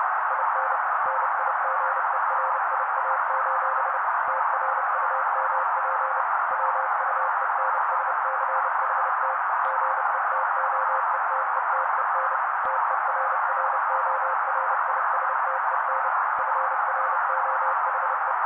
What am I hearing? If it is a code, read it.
Morse code, heard as a keyed tone, about 30 wpm, 530 Hz. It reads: IN NIGERIA: TRUMP WUENSCHT GETOETETEN TERRORISTEN FROHE
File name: daily / 29wpm noise